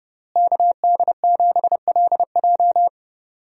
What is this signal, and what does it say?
Morse code, heard as a keyed tone, about 29 wpm, 700 Hz. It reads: KD7LJ
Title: USA Call Signs - 5 Characters - Repeated 3x - 30wpm